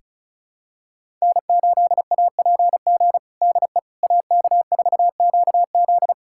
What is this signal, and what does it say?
Morse code, heard as a keyed tone, about 36 wpm, 700 Hz. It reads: N8APG DE AK4QZ
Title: USA Callsign DE Callsign - 5 Characters - 35wpm